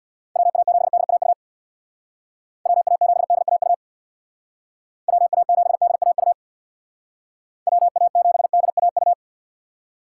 Morse code, heard as a keyed tone, about 62 wpm, 700 Hz. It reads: WA7DAU WA7DAU WA7DAU WA7DAU
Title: Speed Racing - USA 2x3 Call Signs 50wpm